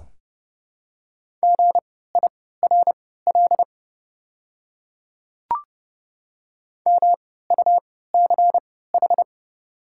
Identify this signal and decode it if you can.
Morse code, heard as a keyed tone, about 29 wpm, spaced as for 17 wpm, 690 Hz. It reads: GIRL MUCH
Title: 3x Character Spacing - Top 500 Words - 30wpm